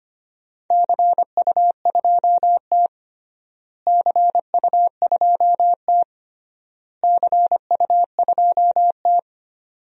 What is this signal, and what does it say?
Morse code, heard as a keyed tone, about 25 wpm, 700 Hz. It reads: CU2T CU2T CU2T
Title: DXCC-100 Call Signs - 4 Characters - Repeated 3x - 25wpm